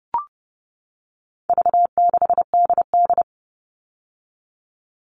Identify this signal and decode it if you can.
Morse code, heard as a keyed tone, about 30 wpm, 700 Hz. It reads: V6DD
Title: All Cut Numbers - 4-Digit 30wpm